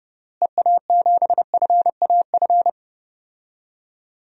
Morse code, heard as a keyed tone, about 30 wpm, 700 Hz. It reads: EA7FAF
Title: DXCC-100 Call Signs - 6 Characters - Repeated 3x - 30wpm